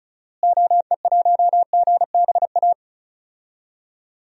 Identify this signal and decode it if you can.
Morse code, heard as a keyed tone, about 34 wpm, 700 Hz. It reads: OE1GBA
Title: DXCC-100 Call Signs - 6 Characters - Repeated 3x - 35wpm